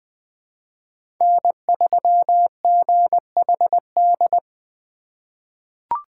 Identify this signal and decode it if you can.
Morse code, heard as a keyed tone, about 20 wpm, 700 Hz. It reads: N3GHD
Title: USA Call Signs - 5 Characters - 3x Character Spacing - 20wpm